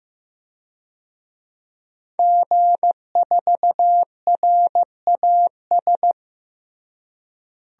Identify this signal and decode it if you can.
Morse code, heard as a keyed tone, about 15 wpm, 700 Hz. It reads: G4RAS
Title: DXCC-100 Call Signs - 5 Characters - Repeated 2x - 15wpm